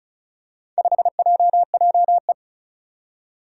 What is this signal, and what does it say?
Morse code, heard as a keyed tone, about 34 wpm, 700 Hz. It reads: 5JJE
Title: Four Challenging Letter-Number Combinations 35wpm